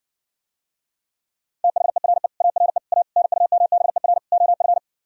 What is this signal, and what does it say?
Morse code, heard as a keyed tone, about 60 wpm, 700 Hz. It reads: THERE ARE A NUMBER OF